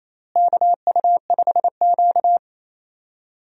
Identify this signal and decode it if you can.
Morse code, heard as a keyed tone, about 28 wpm, 700 Hz. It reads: KU5Q
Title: USA Call Signs - 4 Characters - Repeated 3x - 28wpm